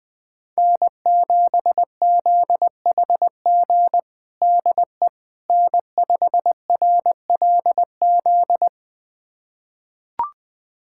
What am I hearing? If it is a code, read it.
Morse code, heard as a keyed tone, about 21 wpm, 700 Hz. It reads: N7ZHG DE N5RLZ
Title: USA Callsign DE Callsign - 5 Characters - 20wpm